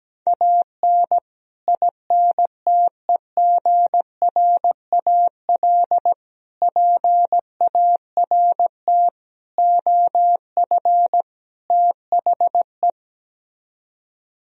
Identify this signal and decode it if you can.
Morse code, heard as a keyed tone, about 17 wpm, 700 Hz. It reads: AN INTEGRAL PART OF THE
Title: Speed-Racing - Sets of 5 Words 17wpm